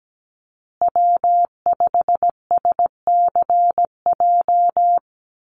Morse code, heard as a keyed tone, about 17 wpm, 700 Hz. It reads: W5SCJ
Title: USA Call Signs Repeated 2x 17wpm